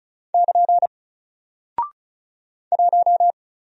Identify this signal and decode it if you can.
Morse code, heard as a keyed tone, about 36 wpm, 700 Hz. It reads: <KN> 1
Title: Single Character (Letter:Number:Punctuation:Prosign) Alt 35wpm